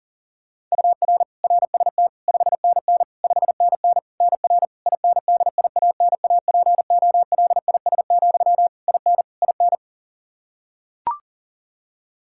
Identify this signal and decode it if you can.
Morse code, heard as a keyed tone, about 40 wpm, 700 Hz. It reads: UR RST 5NN 5NN NR INDIANAPOLIS, IN IN
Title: K1USN SST QSO Practice 40wpm